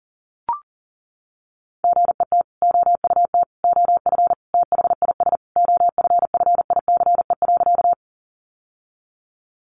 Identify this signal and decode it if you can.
Morse code, heard as a keyed tone, about 37 wpm, 700 Hz. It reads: GET OUT OF THIS OFFICE.
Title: Sentences from Top 1000 Words 40wpm